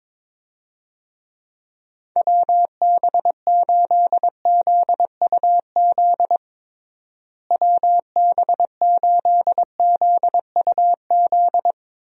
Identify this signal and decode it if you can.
Morse code, heard as a keyed tone, about 22 wpm, 700 Hz. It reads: WB8ZUZ WB8ZUZ
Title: USA Call Signs - 6 Characters - Repeated 2x - 22wpm